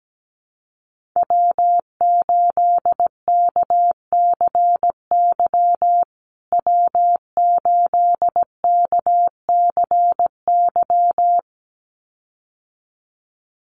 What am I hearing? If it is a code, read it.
Morse code, heard as a keyed tone, about 17 wpm, 700 Hz. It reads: W8KCY W8KCY
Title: USA Call Signs Repeated 2x 17wpm